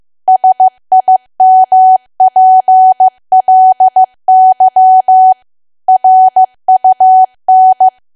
Morse code, heard as a keyed tone, about 15 wpm, 750 Hz. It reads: SIMPLY RUN
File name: cwa / W1AW-Level2 / 15WPM